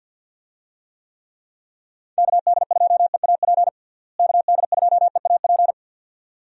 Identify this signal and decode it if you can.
Morse code, heard as a keyed tone, about 46 wpm, 700 Hz. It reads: KD2EAP KD2EAP
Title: USA Call Signs - 6 Characters - Repeated 2x - 50wpm